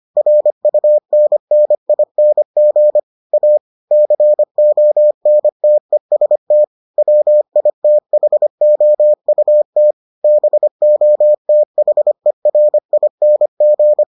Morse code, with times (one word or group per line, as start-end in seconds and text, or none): 0.2-3.0 RUNNING
3.3-3.6 A
3.9-6.6 CONTEST
7.0-9.9 WITHOUT
10.2-14.0 BOTHERING